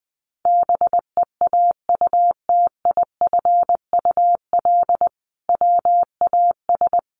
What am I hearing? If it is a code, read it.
Morse code, heard as a keyed tone, about 20 wpm, 700 Hz. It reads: BEAUTIFUL WAS
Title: Sentences from Top 1100 Words 20wpm